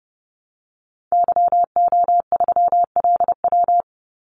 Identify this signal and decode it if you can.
Morse code, heard as a keyed tone, about 29 wpm, 700 Hz. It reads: YO3LW
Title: Call Signs Repeated 2x - 30wpm